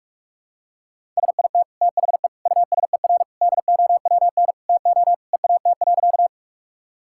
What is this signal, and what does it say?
Morse code, heard as a keyed tone, about 44 wpm, 700 Hz. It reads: SIT THE USER DOWN TO EAT.
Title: Sentences from Top 1000 Words - Review Part 5 - 45wpm